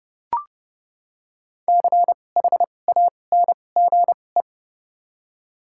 Morse code, heard as a keyed tone, about 31 wpm, spaced as for 22 wpm, 700 Hz. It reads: CHANGE
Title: 2x Character Spacing - Top 200 Words - 30wpm